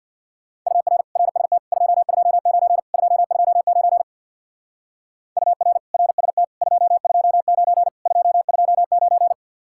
Morse code, heard as a keyed tone, about 60 wpm, 700 Hz. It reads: UR RST 229 229 UR RST 229 229
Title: Speed-Racing Common QSO Phrases Encore - 50wpm